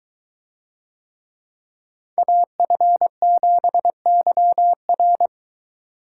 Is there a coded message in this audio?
Yes: Morse code from 2.2 to 5.3 s, heard as a keyed tone, about 23 wpm, 700 Hz. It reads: AF7YR